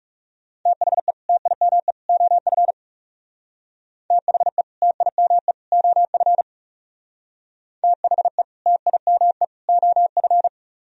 Morse code, heard as a keyed tone, about 40 wpm, 700 Hz. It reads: THE TIME OF THE TIME OF THE TIME OF